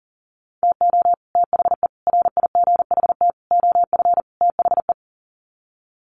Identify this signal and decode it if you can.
Morse code, heard as a keyed tone, about 40 wpm, 700 Hz. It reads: TO THE RIGHT OF THE
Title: Sets of 5 Words Encore 40wpm